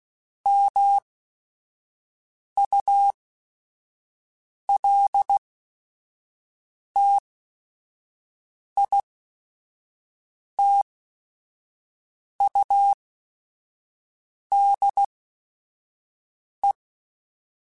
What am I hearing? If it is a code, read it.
Morse code, heard as a keyed tone, about 16 wpm, spaced as for 5 wpm, 800 Hz. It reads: MULTITUDE